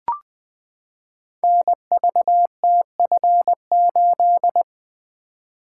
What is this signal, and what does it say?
Morse code, heard as a keyed tone, about 20 wpm, 700 Hz. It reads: NVTF8
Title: Code Groups - 5 Letter:Numbers - 20wpm